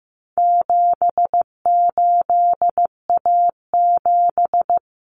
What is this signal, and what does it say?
Morse code, heard as a keyed tone, about 15 wpm, 700 Hz. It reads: 78A7
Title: Common Cut Numbers - 4-Digit 15wpm